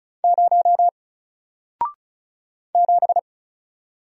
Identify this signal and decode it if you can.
Morse code, heard as a keyed tone, about 36 wpm, 700 Hz. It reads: 0 7
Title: Single Letter-Number 35wpm